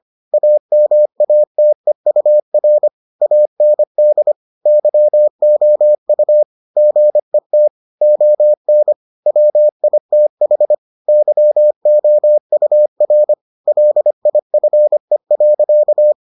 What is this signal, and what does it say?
Morse code, heard as a keyed tone, about 25 wpm, 600 Hz. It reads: AMATEUR AND YOU GET ON WITH YOUR LIFE.